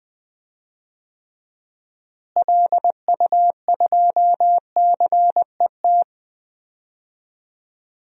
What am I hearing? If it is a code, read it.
Morse code, heard as a keyed tone, about 20 wpm, 700 Hz. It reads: LU2CET
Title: DXCC-100 Call Signs - 6 Characters - Repeated 2x - 20wpm